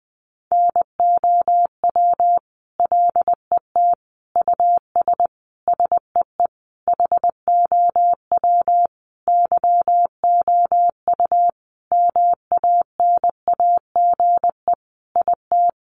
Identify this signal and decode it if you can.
Morse code, heard as a keyed tone, about 20 wpm, 700 Hz. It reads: NOW LET US SEE HOW YOU MANAGE IT
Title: Sentences from Top 1400 Words 20wpm